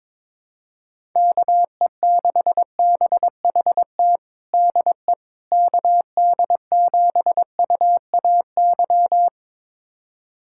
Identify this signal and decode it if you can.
Morse code, heard as a keyed tone, about 22 wpm, 700 Hz. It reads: KE6BHT DE KD7UAY
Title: USA Callsign DE Callsign 22wpm